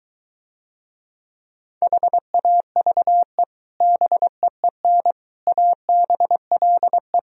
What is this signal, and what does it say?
Morse code, heard as a keyed tone, about 23 wpm, 700 Hz. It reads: HAVE BEEN ABLE